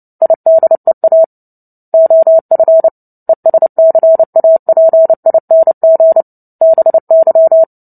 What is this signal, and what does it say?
Morse code, heard as a keyed tone, about 30 wpm, 650 Hz. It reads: IDEA OF ESCAPING BY
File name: Book Two / 03-Chapter 3 - 29 WPM